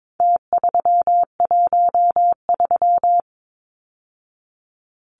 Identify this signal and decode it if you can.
Morse code, heard as a keyed tone, about 22 wpm, 700 Hz. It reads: T313
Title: Numbers - Contest Style - 4-Digit 22wpm